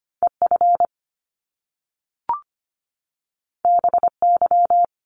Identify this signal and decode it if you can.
Morse code, heard as a keyed tone, about 25 wpm, 700 Hz. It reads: EF BY
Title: Top Two Letter Combinations in Words 25wpm